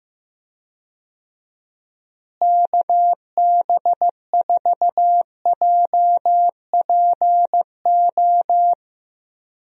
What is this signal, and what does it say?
Morse code, heard as a keyed tone, about 15 wpm, 700 Hz. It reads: KB4JPO
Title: USA Call Signs - 6 Characters - Repeated 3x - 15wpm